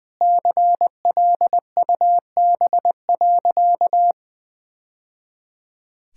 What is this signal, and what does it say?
Morse code, heard as a keyed tone, about 20 wpm, 700 Hz. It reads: CLUB.
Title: Sentences with 3-Word Phrases - Spoken Beginning - 20wpm